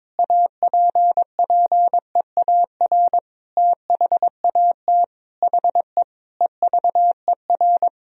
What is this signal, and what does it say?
Morse code, heard as a keyed tone, about 22 wpm, 700 Hz. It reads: APPEAR THAT HE EVER
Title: Sentences from Top 1000 Words - Review Part 2 - 22wpm